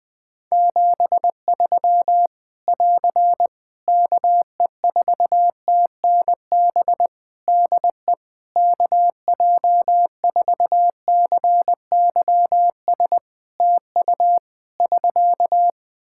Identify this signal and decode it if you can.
Morse code, heard as a keyed tone, about 20 wpm, 700 Hz. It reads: 73 <AR> KE4TNB DE KJ4CYS TU <SK>
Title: Practice QSO - Exchange 4 - Ending - 20wpm